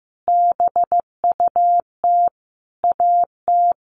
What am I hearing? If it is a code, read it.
Morse code, heard as a keyed tone, about 15 wpm, 700 Hz. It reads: BUT AT
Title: Sentences from Top 1400 Words 15wpm